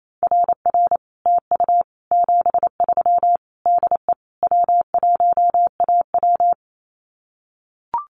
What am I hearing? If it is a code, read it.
Morse code, heard as a keyed tone, about 28 wpm, 700 Hz. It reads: RR TU 73 DE W1AW
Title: Common QSO Phrases Encore 28wpm